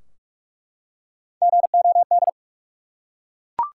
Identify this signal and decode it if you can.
Morse code, heard as a keyed tone, about 46 wpm, 700 Hz. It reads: GOD
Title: Three Letter Words - 45wpm